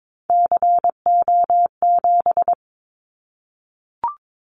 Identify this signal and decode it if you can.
Morse code, heard as a keyed tone, about 22 wpm, 700 Hz. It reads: CO7
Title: Callsign Prefixes for Top 100 DXCC - 22wpm